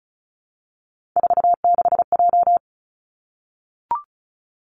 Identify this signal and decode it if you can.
Morse code, heard as a keyed tone, about 36 wpm, 700 Hz. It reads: 46J
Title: Three Challenging Letter-Number Combinations 35wpm